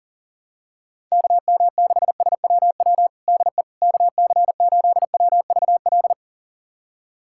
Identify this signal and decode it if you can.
Morse code, heard as a keyed tone, about 37 wpm, 700 Hz. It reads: KM6SWW DE KC8WVL
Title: USA Callsign DE Callsign 40wpm